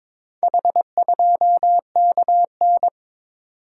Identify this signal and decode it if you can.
Morse code, heard as a keyed tone, about 21 wpm, 700 Hz. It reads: H2KN
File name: practice / 22wpm / Four Letter-Number Combinations - 22wpm